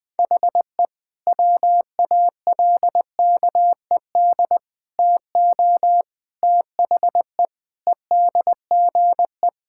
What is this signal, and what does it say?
Morse code, heard as a keyed tone, about 20 wpm, 700 Hz. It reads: HE WALKED TO THE EDGE
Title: Sentences with 5-Word Phrases - Spoken Beginning - 20wpm